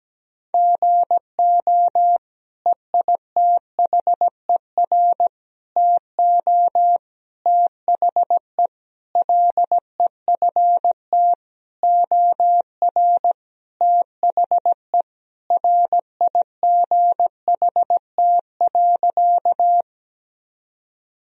Morse code, heard as a keyed tone, about 17 wpm, 700 Hz. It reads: GO EITHER TO THE LEFT OR THE RIGHT.
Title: Sentences from Top 1000 Words - Review Part 4 - 17wpm